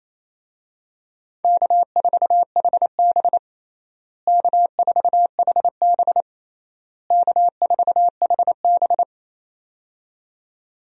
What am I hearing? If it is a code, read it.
Morse code, heard as a keyed tone, about 28 wpm, 700 Hz. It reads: K4HB K4HB K4HB
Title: USA Call Signs - 4 Characters - Repeated 3x - 28wpm